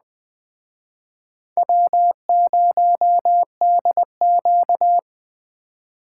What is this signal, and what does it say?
Morse code, heard as a keyed tone, about 20 wpm, 700 Hz. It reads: W0DQ